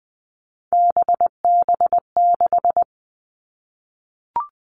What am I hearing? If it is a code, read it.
Morse code, heard as a keyed tone, about 20 wpm, 700 Hz. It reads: BB6
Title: All Cut Numbers - 3-Digit 20wpm